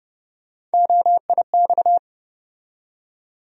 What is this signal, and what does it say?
Morse code, heard as a keyed tone, about 30 wpm, 700 Hz. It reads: OIX
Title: USA Call Sign Postfixes - 3 Letters 30wpm